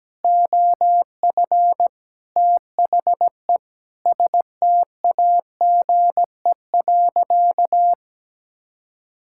Morse code, heard as a keyed tone, about 17 wpm, 700 Hz. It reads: OF THE STAGE.
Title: Sentences from Top 1600 Words 17wpm